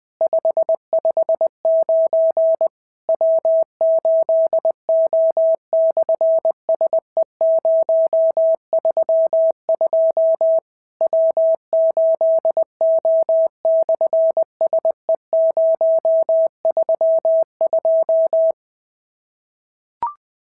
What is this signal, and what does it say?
Morse code, heard as a keyed tone, about 19 wpm, 650 Hz. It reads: 559 W8O/SE032 W8O/SE032
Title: SOTA QSO Practice 20wpm